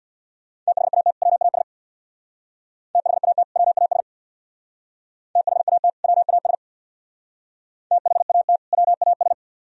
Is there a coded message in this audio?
Yes: Morse code from 0.7 to 9.3 s, heard as a keyed tone, about 64 wpm, 700 Hz. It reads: THAT WAS THAT WAS THAT WAS THAT WAS